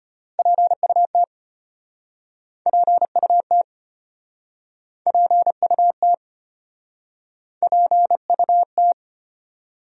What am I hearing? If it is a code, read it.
Morse code, heard as a keyed tone, about 35 wpm, 700 Hz. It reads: PUT PUT PUT PUT